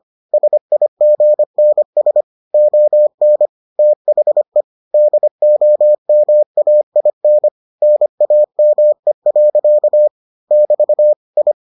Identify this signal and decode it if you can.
Morse code, heard as a keyed tone, about 25 wpm, 600 Hz. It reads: SIGNS ON THE DOMAIN NAME. <BT> I